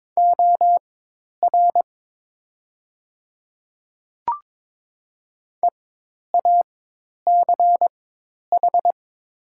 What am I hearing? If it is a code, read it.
Morse code, heard as a keyed tone, about 22 wpm, spaced as for 10 wpm, 700 Hz. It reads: OR EACH
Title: 4x Character Spacing - Top 100 Words - 22wpm